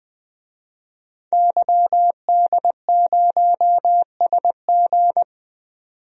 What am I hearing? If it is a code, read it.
Morse code, heard as a keyed tone, about 20 wpm, 700 Hz. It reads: YD0SG
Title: DXCC-100 Call Signs - 5 Characters - 20wpm